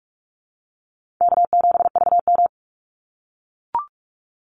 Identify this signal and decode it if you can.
Morse code, heard as a keyed tone, about 44 wpm, 700 Hz. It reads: K7VM
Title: Call Signs - 45wpm